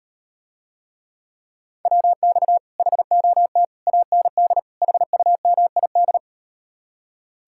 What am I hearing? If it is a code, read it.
Morse code, heard as a keyed tone, about 38 wpm, 700 Hz. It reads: WX HOT AND HUMID